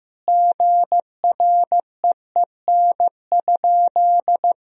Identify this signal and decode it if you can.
Morse code, heard as a keyed tone, about 15 wpm, 700 Hz. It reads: GREEN?
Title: Sentences from Top 1000 Words - Review Part 2 - 15wpm